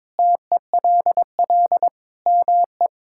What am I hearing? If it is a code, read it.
Morse code, heard as a keyed tone, about 22 wpm, 700 Hz. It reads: TELL ME
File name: practice / 22wpm / Sentences with 4-Word Phrases - 22wpm